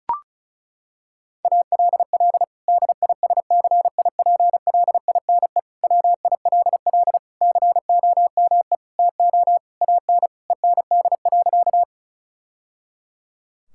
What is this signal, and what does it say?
Morse code, heard as a keyed tone, about 34 wpm, 700 Hz. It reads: ALL DISCIPLINE WILL COME TO AN END.
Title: Sentences from Top 1000 Words - Review Part 6 - 35wpm